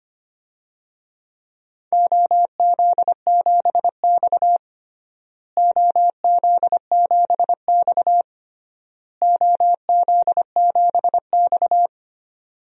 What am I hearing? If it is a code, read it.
Morse code, heard as a keyed tone, about 25 wpm, 700 Hz. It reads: OZ7X OZ7X OZ7X
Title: DXCC-100 Call Signs - 4 Characters - Repeated 3x - 25wpm